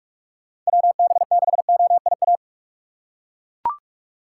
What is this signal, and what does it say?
Morse code, heard as a keyed tone, about 42 wpm, 700 Hz. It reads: WB6OIA
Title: USA Call Signs - 6 Characters - 2x Character Spacing - 45wpm